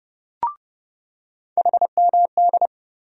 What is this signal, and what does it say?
Morse code, heard as a keyed tone, about 30 wpm, 700 Hz. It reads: HMD
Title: USA Call Sign Postfixes - 3 Letters 30wpm